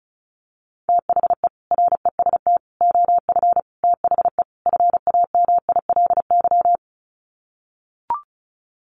Morse code, heard as a keyed tone, about 34 wpm, 700 Hz. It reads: THE REST OF THE FAMILY
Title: Sets of 5 Words Encore2 35wpm